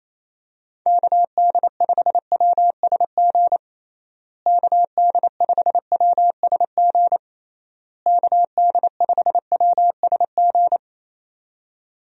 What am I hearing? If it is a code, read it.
Morse code, heard as a keyed tone, about 28 wpm, 700 Hz. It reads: KD5WSG KD5WSG KD5WSG
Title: USA Call Signs - 6 Characters - Repeated 3x - 28wpm